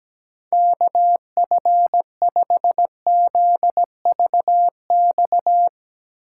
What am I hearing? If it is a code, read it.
Morse code, heard as a keyed tone, about 17 wpm, 700 Hz. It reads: KF5ZVX
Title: USA Call Signs - 6 Characters - 3x Character Spacing - 17wpm